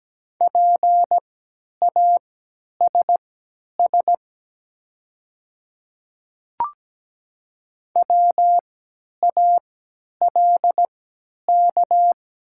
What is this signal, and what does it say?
Morse code, heard as a keyed tone, about 17 wpm, spaced as for 10 wpm, 700 Hz. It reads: PASS WALK
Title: 3x Character Spacing - Top 500 Words - 17wpm